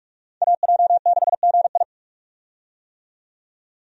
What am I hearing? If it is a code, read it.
Morse code, heard as a keyed tone, about 44 wpm, 700 Hz. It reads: AJ6GI
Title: USA Call Signs - 5 Characters - 45wpm